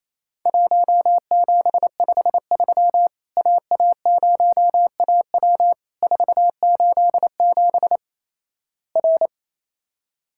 Morse code, heard as a keyed tone, about 28 wpm, 700 Hz. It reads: 1753 AA0AW 487 R
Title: Worked All Europe DX Contest - QTC Exchange - 28wpm